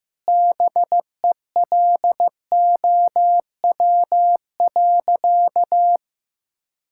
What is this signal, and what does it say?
Morse code, heard as a keyed tone, about 15 wpm, 700 Hz. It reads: BELOW.
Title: Sentences from Top 1000 Words - Review Part 5 - 15wpm